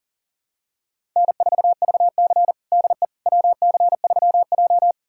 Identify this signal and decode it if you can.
Morse code, heard as a keyed tone, about 40 wpm, 700 Hz. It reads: N4VC DE WC3J
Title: USA Callsign DE Callsign - 4 Characters - 40wpm